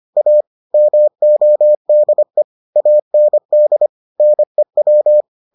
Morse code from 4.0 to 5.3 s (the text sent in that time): NEW